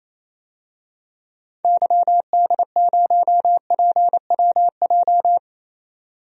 Morse code, heard as a keyed tone, about 28 wpm, 700 Hz. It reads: YD0PWJ
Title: DXCC-100 Call Signs - 6 Characters - 28wpm